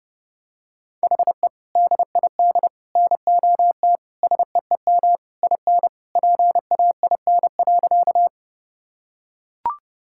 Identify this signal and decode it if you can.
Morse code, heard as a keyed tone, about 32 wpm, 700 Hz. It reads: HE DID NOT SEEM IN PAIN.
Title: Sentences from Top 1600 Words 30wpm